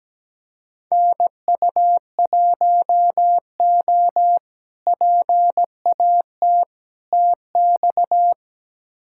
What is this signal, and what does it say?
Morse code, heard as a keyed tone, about 17 wpm, 700 Hz. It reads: NU1O PAT TX
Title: K1USN SST QSO Practice - 17wpm